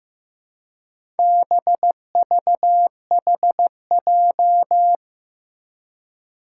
Morse code, heard as a keyed tone, about 15 wpm, 700 Hz. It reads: BVHJ
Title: Four Challenging Letter-Number Combinations 15wpm